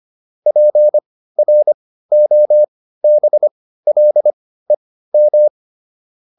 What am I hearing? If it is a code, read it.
Morse code, heard as a keyed tone, about 25 wpm, spaced as for 15 wpm, 600 Hz. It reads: PROBLEM